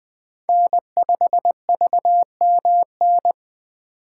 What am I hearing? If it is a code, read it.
Morse code, heard as a keyed tone, about 20 wpm, 700 Hz. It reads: N5VMN
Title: 2023 POTA Hunter Call Signs - 20wpm